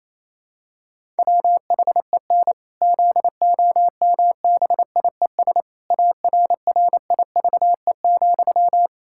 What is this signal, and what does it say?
Morse code, heard as a keyed tone, about 28 wpm, 700 Hz. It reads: WHEN ZOMBIES ARRIVE,
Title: Pangram Challenge v2 - 28wpm